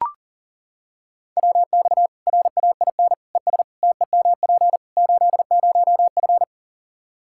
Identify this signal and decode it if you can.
Morse code, heard as a keyed tone, about 40 wpm, 700 Hz. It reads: WX RAIN ES TEMP 80F
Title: Practice QSO - Exchange 2 - Info - 40wpm